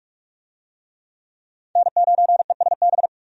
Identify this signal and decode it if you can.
Morse code, heard as a keyed tone, about 46 wpm, 700 Hz. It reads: N9ESB
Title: USA Call Signs - 5 Characters - 45wpm